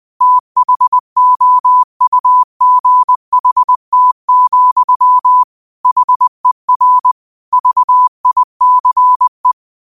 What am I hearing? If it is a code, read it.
Morse code, heard as a keyed tone, about 20 wpm, 1000 Hz. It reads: THOUGHT, HER VICE
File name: qotdfiles / 20wpm